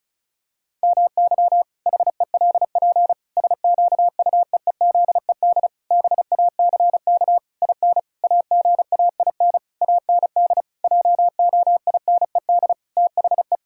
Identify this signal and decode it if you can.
Morse code, heard as a keyed tone, about 34 wpm, 700 Hz. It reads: MY HELP SQUEEZED BACK IN AGAIN AND JOINED THE
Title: Pangram Challenge v2 - 35wpm